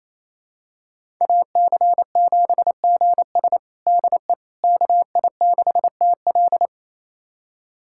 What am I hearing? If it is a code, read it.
Morse code, heard as a keyed tone, about 28 wpm, 700 Hz. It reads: AC7GS DE KI6TL
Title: USA Callsign DE Callsign - 5 Characters - 28wpm